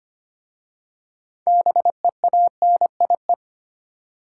Morse code, heard as a keyed tone, about 25 wpm, 700 Hz. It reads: BEANIE
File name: practice / 25wpm / Difficult Words with Hints - 25wpm